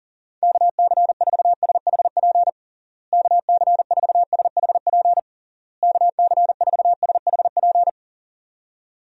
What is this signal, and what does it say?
Morse code, heard as a keyed tone, about 40 wpm, 700 Hz. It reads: KC4SHP KC4SHP KC4SHP
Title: USA Call Signs - 6 Characters - Repeated 3x - 40wpm